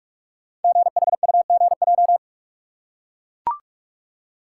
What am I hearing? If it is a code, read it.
Morse code, heard as a keyed tone, about 44 wpm, 700 Hz. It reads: GHUGJ
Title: Code Groups - 5 Letters - 45wpm